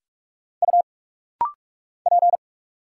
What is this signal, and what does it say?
Morse code, heard as a keyed tone, about 44 wpm, 700 Hz. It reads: U P
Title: Single Letter-Number Alt - Rapid-Fire 45wpm